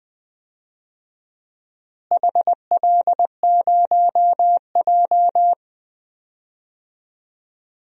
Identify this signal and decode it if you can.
Morse code, heard as a keyed tone, about 20 wpm, 700 Hz. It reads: HL0J
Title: DXCC-100 Call Signs - 4 Characters - Repeated 3x - 20wpm